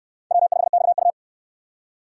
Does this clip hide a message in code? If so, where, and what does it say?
Yes: Morse code from 0.3 to 1.1 s, heard as a keyed tone, about 69 wpm, 700 Hz. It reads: W5XU